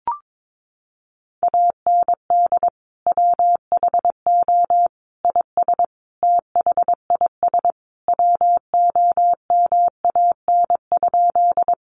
Morse code, heard as a keyed tone, about 22 wpm, 700 Hz. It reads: AND WHO IS THIS WOMAN?
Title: Sentences from Top 1100 Words 22wpm